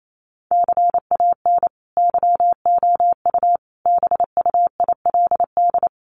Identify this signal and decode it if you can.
Morse code, heard as a keyed tone, about 28 wpm, 700 Hz. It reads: CAN YOU BUILD
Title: Sentences from Top 1400 Words 28wpm